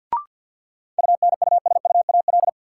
Morse code, heard as a keyed tone, about 52 wpm, 700 Hz. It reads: UNUSUAL
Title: Sentences from Top 1400 Words - New Word Drill - 50wpm